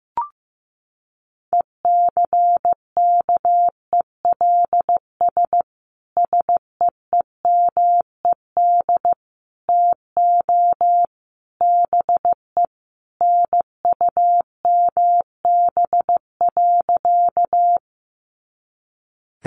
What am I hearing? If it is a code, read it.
Morse code, heard as a keyed tone, about 15 wpm, 700 Hz. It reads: ECKELS SEEMED TO BE NUMB.
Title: A Sound Of Thunder 15wpm